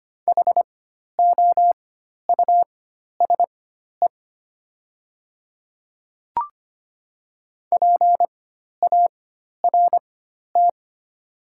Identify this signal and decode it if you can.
Morse code, heard as a keyed tone, about 25 wpm, spaced as for 12 wpm, 700 Hz. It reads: HOUSE PART
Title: 4x Character Spacing - Top 200 Words - 25wpm